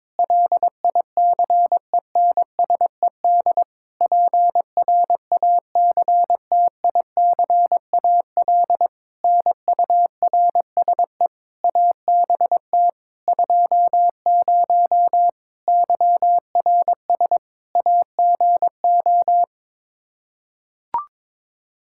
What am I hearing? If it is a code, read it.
Morse code, heard as a keyed tone, about 22 wpm, 700 Hz. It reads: LICENSED PRACTICAL NURSE ABT 20 YRS AGO
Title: Practice QSO - Exchange 3 - More Info - 22wpm